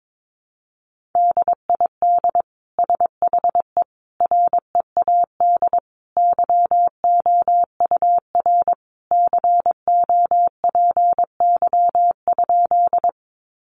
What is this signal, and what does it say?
Morse code, heard as a keyed tone, about 22 wpm, 700 Hz. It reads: DID SHE READ YOUR COPY?
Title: Sentences from Top 1400 Words 22wpm